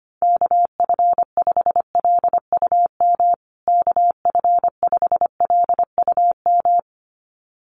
Morse code, heard as a keyed tone, about 25 wpm, 700 Hz. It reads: KF5LUM KF5LUM
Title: USA Call Signs Repeated 2x 25wpm